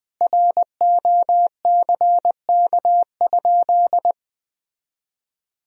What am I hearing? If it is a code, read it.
Morse code, heard as a keyed tone, about 20 wpm, 700 Hz. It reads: ROCK?
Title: Sentences from Top 1000 Words - Review Part 2 - 20wpm